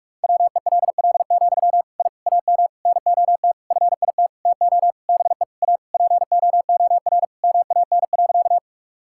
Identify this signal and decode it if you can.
Morse code, heard as a keyed tone, about 46 wpm, 700 Hz. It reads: WELL, I AM NOT FIT TO BE A POOR MAN.
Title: Sentences from Top 1000 Words - Review Part 5 - 45wpm